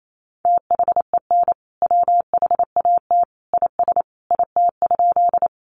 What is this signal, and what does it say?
Morse code, heard as a keyed tone, about 28 wpm, 700 Hz. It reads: THEN WHAT IS IT?
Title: Sentences from Top 100 Words 28wpm